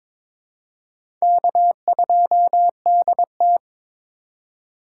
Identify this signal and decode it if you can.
Morse code, heard as a keyed tone, about 22 wpm, 700 Hz. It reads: K2DT
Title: USA Call Signs - 4 Characters - 3x Character Spacing - 22wpm